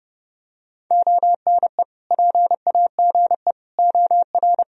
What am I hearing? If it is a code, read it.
Morse code, heard as a keyed tone, about 32 wpm, 700 Hz. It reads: ONE PAGE OR
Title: Sentences from Top 1000 Words - Review Part 1 - 30wpm